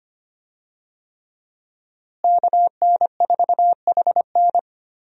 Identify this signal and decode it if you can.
Morse code, heard as a keyed tone, about 25 wpm, 700 Hz. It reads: KN4HN
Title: USA Call Signs - 5 Characters - Repeated 3x - 25wpm